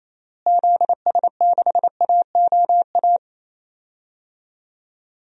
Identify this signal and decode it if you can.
Morse code, heard as a keyed tone, about 28 wpm, 700 Hz. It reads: ZS6AOA
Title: DXCC-100 Call Signs - 6 Characters - Repeated 3x - 28wpm